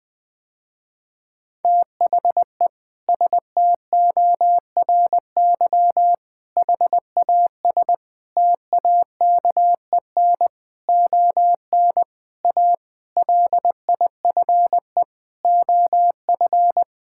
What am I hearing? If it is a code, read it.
Morse code, heard as a keyed tone, about 20 wpm, 700 Hz. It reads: THE STORY HAS TAKEN ON A LIFE OF
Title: Sentences with 7-Word Phrases - Spoken Beginning - 20wpm